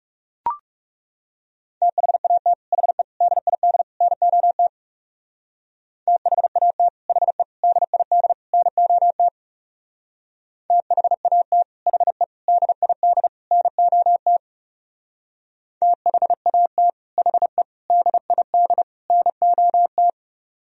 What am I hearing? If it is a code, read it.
Morse code, heard as a keyed tone, about 39 wpm, 700 Hz. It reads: THAT HE DID NOT THAT HE DID NOT THAT HE DID NOT THAT HE DID NOT